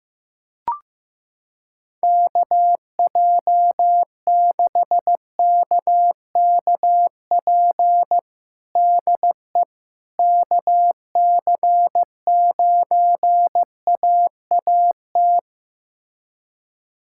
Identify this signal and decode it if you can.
Morse code, heard as a keyed tone, about 15 wpm, 700 Hz. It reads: KJ6KKP DE KC9AAT
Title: USA Callsign DE Callsign - 6 Characters - 15wpm